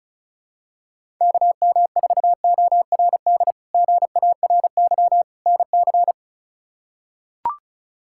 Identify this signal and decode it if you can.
Morse code, heard as a keyed tone, about 36 wpm, 700 Hz. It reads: KM4ORD GARY NC
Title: K1USN SST QSO Practice - 35wpm